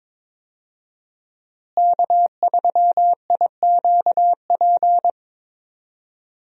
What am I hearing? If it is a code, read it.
Morse code, heard as a keyed tone, about 21 wpm, 700 Hz. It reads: K3IQP
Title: USA Call Signs - 5 Characters - Repeated 3x - 22wpm